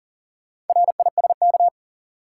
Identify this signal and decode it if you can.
Morse code, heard as a keyed tone, about 40 wpm, 700 Hz. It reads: RISK
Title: Sentences from Top 600 Words - New Word Drill - 40wpm